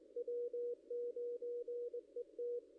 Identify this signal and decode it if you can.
Morse code, heard as a keyed tone, about 21 wpm, 480 Hz. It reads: W9ET
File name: contests / ss-cw-k5tr-op-n5rz -1700